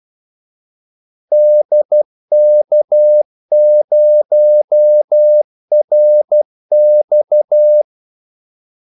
Morse code, heard as a keyed tone, about 12 wpm, 600 Hz. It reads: DK0RX
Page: 2 ekstra ordmellemrum mellem kaldesignalerne